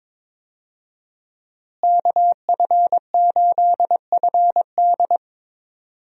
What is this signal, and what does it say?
Morse code, heard as a keyed tone, about 22 wpm, 700 Hz. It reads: KF8FD
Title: USA Call Signs - 5 Characters - Repeated 3x - 22wpm